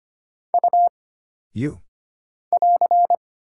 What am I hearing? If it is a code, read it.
Morse code, heard as a keyed tone, about 25 wpm, 700 Hz. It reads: U <AR>
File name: Single Character (Letter:Number:Punctuation:Prosign) - Mind-Melt - 25wpm